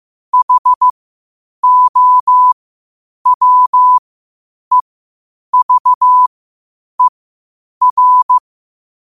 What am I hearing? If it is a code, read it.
Morse code, heard as a keyed tone, about 15 wpm, spaced as for 8 wpm, 1000 Hz. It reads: HOWEVER
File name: 8wpm